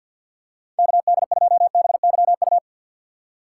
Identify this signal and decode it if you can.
Morse code, heard as a keyed tone, about 52 wpm, 700 Hz. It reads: KD2BYU
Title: USA Call Signs - 6 Characters - 6x Character Spacing - 50wpm